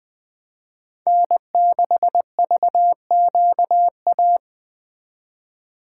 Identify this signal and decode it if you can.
Morse code, heard as a keyed tone, about 21 wpm, 700 Hz. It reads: N6VQA